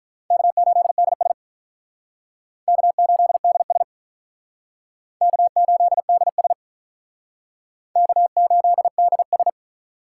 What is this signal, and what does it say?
Morse code, heard as a keyed tone, about 44 wpm, 700 Hz. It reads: K8DS K8DS K8DS K8DS